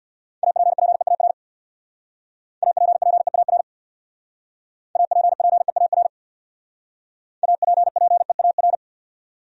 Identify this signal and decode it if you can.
Morse code, heard as a keyed tone, about 66 wpm, 700 Hz. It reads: APPEAR APPEAR APPEAR APPEAR